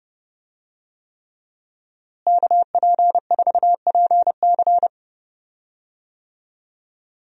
Morse code, heard as a keyed tone, about 30 wpm, 700 Hz. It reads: KP4PC
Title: DXCC-100 Call Signs - 5 Characters - Repeated 3x - 30wpm